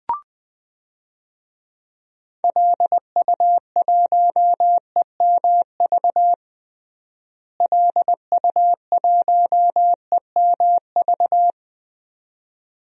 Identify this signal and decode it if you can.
Morse code, heard as a keyed tone, about 20 wpm, 700 Hz. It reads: LU1EMV LU1EMV
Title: DXCC-100 Call Signs - 6 Characters - Repeated 2x - 20wpm